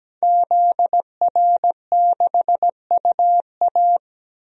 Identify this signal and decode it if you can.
Morse code, heard as a keyed tone, about 17 wpm, 700 Hz. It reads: ZR6UA
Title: DXCC-100 Call Signs - 5 Characters - 17wpm